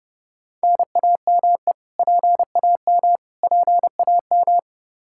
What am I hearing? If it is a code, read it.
Morse code, heard as a keyed tone, about 30 wpm, 700 Hz. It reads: NAME PAM PAM
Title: Practice QSO - Exchange 1 - Beginning - 30wpm